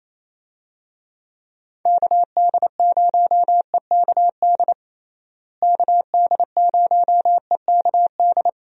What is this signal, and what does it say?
Morse code, heard as a keyed tone, about 28 wpm, 700 Hz. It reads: KD0EKD KD0EKD
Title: USA Call Signs - 6 Characters - Repeated 3x - 28wpm